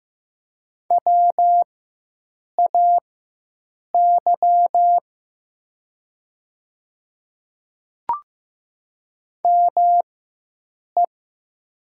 Morse code, heard as a keyed tone, about 15 wpm, spaced as for 7 wpm, 700 Hz. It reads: WAY ME
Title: 4x Character Spacing - Top 300 Words - 15wpm